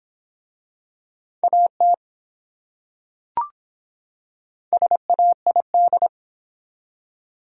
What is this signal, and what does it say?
Morse code, heard as a keyed tone, about 24 wpm, 700 Hz. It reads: AT SAID